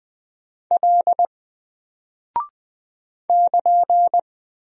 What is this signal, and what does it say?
Morse code, heard as a keyed tone, about 20 wpm, 700 Hz. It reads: L <KN>
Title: Single Character (Letter:Number:Punctuation:Prosign) Alt 20wpm